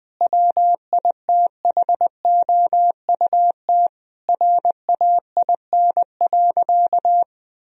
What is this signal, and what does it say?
Morse code, heard as a keyed tone, about 20 wpm, 700 Hz. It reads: WITHOUT RAIN.
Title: Sentences from Top 1000 Words - Review Part 3 - 20wpm